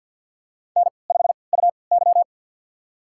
Morse code, heard as a keyed tone, about 50 wpm, spaced as for 28 wpm, 700 Hz. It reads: N5UY
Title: USA Call Signs - 4 Characters - 3x Character Spacing - 50wpm